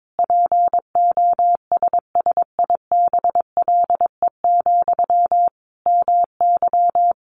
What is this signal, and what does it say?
Morse code, heard as a keyed tone, about 22 wpm, 700 Hz. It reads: POSSIBLE, MY
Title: Sentences from Top 800 Words 22wpm